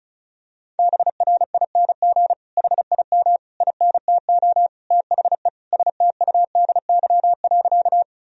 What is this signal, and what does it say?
Morse code, heard as a keyed tone, about 34 wpm, 700 Hz. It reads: BRING HIM INTO THE STUDY.
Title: Sentences from Top 1000 Words - Review Part 1 - 35wpm